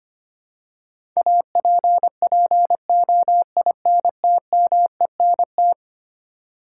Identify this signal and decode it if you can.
Morse code, heard as a keyed tone, about 25 wpm, 700 Hz. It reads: APPOINTMENT
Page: morse-code-ninja-practice-25wpm